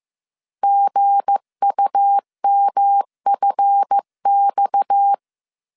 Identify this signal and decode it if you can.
Morse code, heard as a keyed tone, about 15 wpm, 800 Hz. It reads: GUMFX
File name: CW / DET / 1 9 1500C S A 14wpm 3222